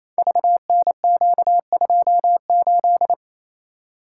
Morse code, heard as a keyed tone, about 27 wpm, 700 Hz. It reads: VNQ28
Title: Code Groups - 5 Letter:Numbers - 28wpm